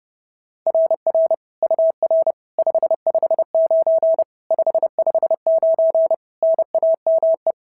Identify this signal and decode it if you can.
Morse code, heard as a keyed tone, about 32 wpm, 660 Hz. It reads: RR UR 559 559 NAME
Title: SOTA QSO Practice 30wpm